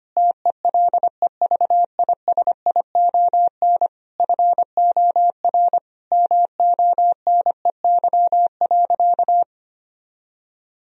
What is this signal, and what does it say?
Morse code, heard as a keyed tone, about 25 wpm, 700 Hz. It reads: TELEVISION FOR MONEY.
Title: Sentences from Top 1000 Words - Review Part 4 - 25wpm